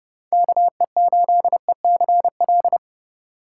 Morse code, heard as a keyed tone, about 30 wpm, 700 Hz. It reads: KE8ECL